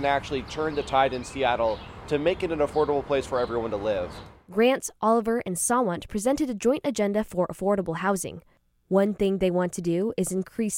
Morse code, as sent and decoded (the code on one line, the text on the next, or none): none